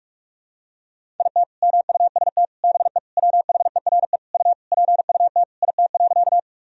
Morse code, heard as a keyed tone, about 41 wpm, 700 Hz. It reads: IT MUST BE WHERE U PUT IT.
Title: Sentences from Top 200 Words and Abbreviations 45wpm